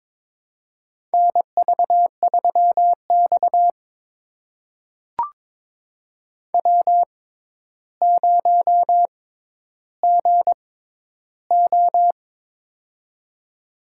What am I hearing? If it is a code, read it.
Morse code, heard as a keyed tone, about 22 wpm, spaced as for 11 wpm, 700 Hz. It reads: NV3X W0GO
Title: USA Call Signs - 4 Characters - 6x Character Spacing - 22wpm